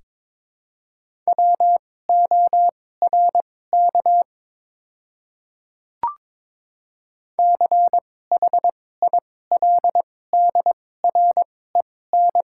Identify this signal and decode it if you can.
Morse code, heard as a keyed tone, about 22 wpm, spaced as for 16 wpm, 700 Hz. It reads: WORK CHILDREN
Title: 2x Character Spacing - Top 500 Words - 22wpm